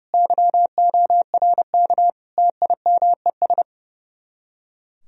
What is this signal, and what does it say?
Morse code, heard as a keyed tone, about 35 wpm, 700 Hz. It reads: YORK TIMES
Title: Speed-Racing - Sets of 4 Words 30wpm